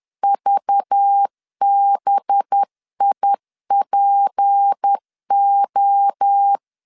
Morse code, heard as a keyed tone, about 10 wpm, 800 Hz. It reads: VBIPO
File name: LEZIONI / 6 PKX0 / 6 PKX0 10wpm 2707